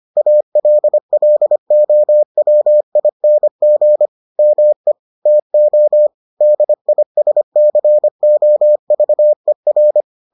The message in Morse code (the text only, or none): ALLOWING ME TO DISCOVER